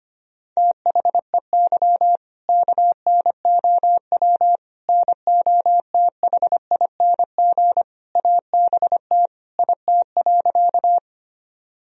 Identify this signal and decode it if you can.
Morse code, heard as a keyed tone, about 25 wpm, 700 Hz. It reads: THEY KNOW NOTHING ABT IT.
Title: Sentences from Top 500 Words and Abbreviations 25wpm